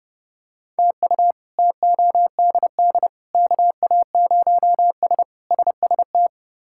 Morse code, heard as a keyed tone, about 31 wpm, 700 Hz. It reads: TU TODD KA0S SST
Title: K1USN SST QSO Practice - 30wpm